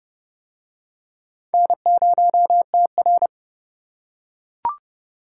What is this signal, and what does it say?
Morse code, heard as a keyed tone, about 30 wpm, 700 Hz. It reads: N0TR